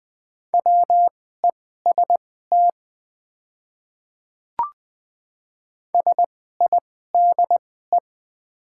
Morse code, heard as a keyed tone, about 21 wpm, spaced as for 15 wpm, 700 Hz. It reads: WEST SIDE
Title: 2x Character Spacing - Top 500 Words - 20wpm